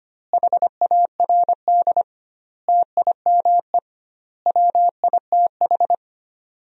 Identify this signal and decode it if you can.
Morse code, heard as a keyed tone, about 25 wpm, 700 Hz. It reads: HARD TIME WITH
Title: 2x Word-Spacing - Sentences from Top 300 Words - 25wpm